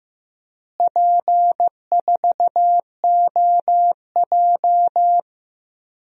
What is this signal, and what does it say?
Morse code, heard as a keyed tone, about 15 wpm, 700 Hz. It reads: P4OJ
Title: DXCC-100 Call Signs - 4 Characters - Repeated 3x - 15wpm